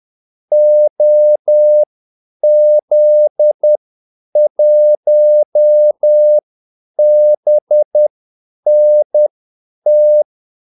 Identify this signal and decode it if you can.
Morse code, heard as a keyed tone, about 10 wpm, spaced as for 8 wpm, 600 Hz. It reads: OZ1BNT
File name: Callsigns-10-8wpm